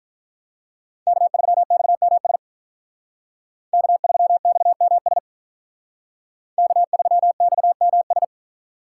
Morse code, heard as a keyed tone, about 45 wpm, 700 Hz. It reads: K3XMS K3XMS K3XMS